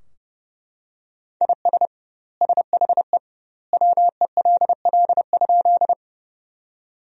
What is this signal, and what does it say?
Morse code, heard as a keyed tone, about 30 wpm, 700 Hz. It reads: IS SHE WELL?
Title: 2x Word-Spacing - Sentences from Top 200 Words - 30wpm